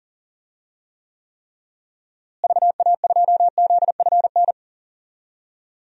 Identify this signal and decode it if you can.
Morse code, heard as a keyed tone, about 40 wpm, 700 Hz. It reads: VA2ZFN
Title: DXCC-100 Call Signs - 6 Characters - Repeated 3x - 40wpm